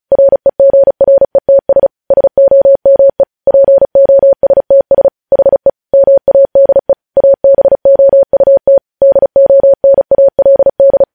Morse code, heard as a keyed tone, about 36 wpm, 550 Hz. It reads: REGRETS SOME POSTS HE MADE ABOUT DONALD
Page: news headlines in morse code at 35 wpm